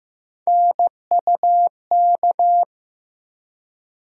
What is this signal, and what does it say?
Morse code, heard as a keyed tone, about 15 wpm, 700 Hz. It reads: NUK
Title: USA Call Sign Postfixes - 3 Letters 15wpm